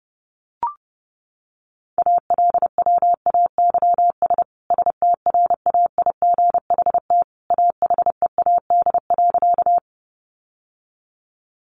Morse code, heard as a keyed tone, about 28 wpm, 700 Hz. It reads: ALWAYS STRAIGHT AHEAD.
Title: Sentences from Top 1100 Words 30wpm